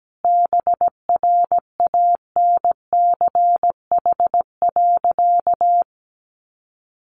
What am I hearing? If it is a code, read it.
Morse code, heard as a keyed tone, about 17 wpm, 700 Hz. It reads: BRANCH.
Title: A Sound Of Thunder 17wpm